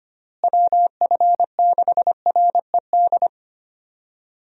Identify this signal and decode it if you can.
Morse code, heard as a keyed tone, about 25 wpm, 700 Hz. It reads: WF6RED
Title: USA Call Signs - 6 Characters - Repeated 3x - 25wpm